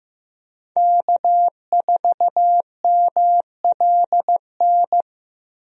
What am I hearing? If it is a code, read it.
Morse code, heard as a keyed tone, about 15 wpm, 700 Hz. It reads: K4MLN
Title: USA Call Signs - 5 Characters - Repeated 3x - 15wpm